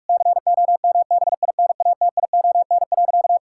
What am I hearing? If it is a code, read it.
Morse code, heard as a keyed tone, about 46 wpm, 700 Hz. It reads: COMBINATION.
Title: Sentences from Top 1000 Words - Review Part 6 - 45wpm